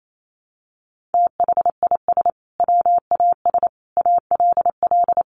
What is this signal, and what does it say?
Morse code, heard as a keyed tone, about 28 wpm, 700 Hz. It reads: THIS WAS ALL
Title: Sentences from Top 1400 Words 28wpm